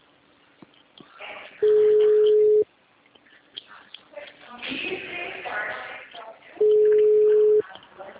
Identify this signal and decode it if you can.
Ringback tone (Europe, 425 Hz), with speech.